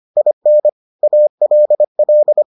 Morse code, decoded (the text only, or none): IN ALL